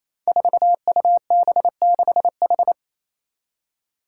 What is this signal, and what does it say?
Morse code, heard as a keyed tone, about 28 wpm, 700 Hz. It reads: 4UB6H
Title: Five Challenging Letter-Number Combinations 28wpm